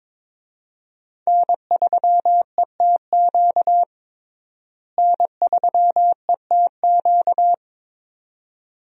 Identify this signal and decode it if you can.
Morse code, heard as a keyed tone, about 22 wpm, 700 Hz. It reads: N3ETQ N3ETQ
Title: USA Call Signs - 5 Characters - Repeated 2x - 22wpm